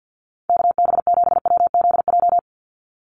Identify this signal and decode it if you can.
Morse code, heard as a keyed tone, about 49 wpm, 700 Hz. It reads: KB7WZJ
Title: USA Call Signs 50wpm